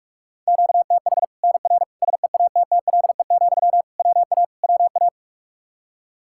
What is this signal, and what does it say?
Morse code, heard as a keyed tone, about 41 wpm, 700 Hz. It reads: QTH NR SEATTLE, WA WA
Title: Practice QSO - Exchange 1 - Beginning - 45wpm